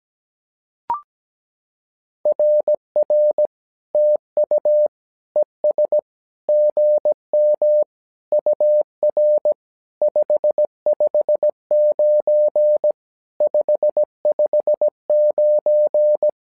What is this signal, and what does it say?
Morse code, heard as a keyed tone, about 17 wpm, 600 Hz. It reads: RR TU ES GM UR 559 559
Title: SOTA QSO Practice 17wpm